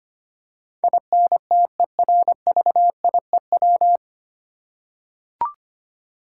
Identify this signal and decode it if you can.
Morse code, heard as a keyed tone, about 25 wpm, 700 Hz. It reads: INTERVIEW
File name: Nine Letter Words - 25wpm